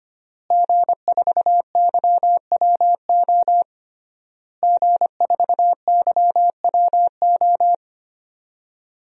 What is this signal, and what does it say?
Morse code, heard as a keyed tone, about 25 wpm, 700 Hz. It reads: G4YWO G4YWO
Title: DXCC-100 Call Signs - 5 Characters - Repeated 3x - 25wpm